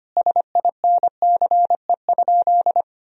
Morse code, heard as a keyed tone, about 25 wpm, 700 Hz. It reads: SINCE?
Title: Sentences from Top 1000 Words - Review Part 4 - 25wpm